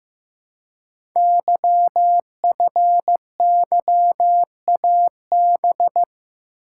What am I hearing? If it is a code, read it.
Morse code, heard as a keyed tone, about 15 wpm, 700 Hz. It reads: YFYAB
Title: Code Groups - 5 Letter:Numbers - 15wpm